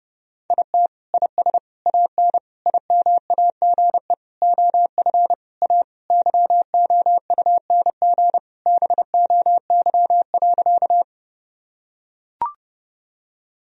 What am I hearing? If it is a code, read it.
Morse code, heard as a keyed tone, about 30 wpm, 700 Hz. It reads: IT IS AN IMAGE OF A YOUNG BOY.
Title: Sentences from Top 1000 Words - Review Part 4 - 30wpm